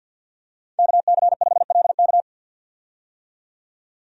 Morse code, heard as a keyed tone, about 51 wpm, 700 Hz. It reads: KC5LK
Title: USA Call Signs - 5 Characters - 50wpm